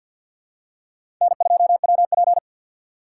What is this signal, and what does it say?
Morse code, heard as a keyed tone, about 51 wpm, 700 Hz. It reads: N2WP
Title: USA Call Signs - 4 Characters - 2x Character Spacing - 50wpm